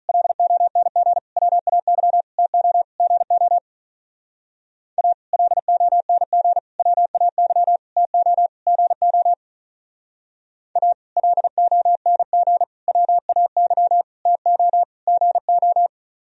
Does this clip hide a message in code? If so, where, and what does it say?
Yes: Morse code from 0.1 to 15.9 s, heard as a keyed tone, about 44 wpm, 700 Hz. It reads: LONG WAY TO GO A LONG WAY TO GO A LONG WAY TO GO